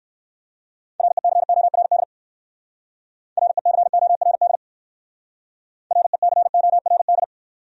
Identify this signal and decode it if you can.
Morse code, heard as a keyed tone, about 55 wpm, 700 Hz. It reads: RECORD RECORD RECORD
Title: Speed-Racing Top 500 Words 45wpm